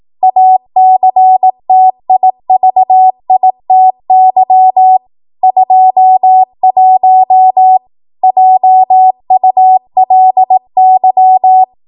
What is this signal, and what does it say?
Morse code, heard as a keyed tone, about 18 wpm, 750 Hz. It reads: ACTIVITY 21 JULY
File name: Morse / 18WPM